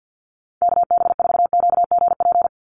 Morse code, heard as a keyed tone, about 50 wpm, 700 Hz. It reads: KB4QGP
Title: USA Call Signs 50wpm